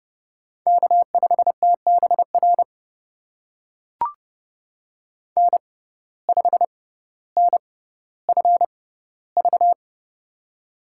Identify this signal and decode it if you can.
Morse code, heard as a keyed tone, about 31 wpm, spaced as for 15 wpm, 700 Hz. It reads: K5TBR N5NFV
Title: USA Call Signs - 5 Characters - 6x Character Spacing - 30wpm